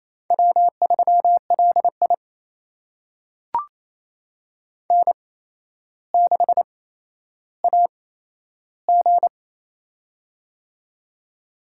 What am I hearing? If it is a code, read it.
Morse code, heard as a keyed tone, about 28 wpm, spaced as for 12 wpm, 700 Hz. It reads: W3LI N6AG
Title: USA Call Signs - 8x Character Spacing - 28wpm